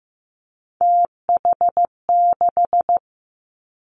Morse code, heard as a keyed tone, about 15 wpm, 700 Hz. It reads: TH6
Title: Three Challenging Letter-Number Combinations 15wpm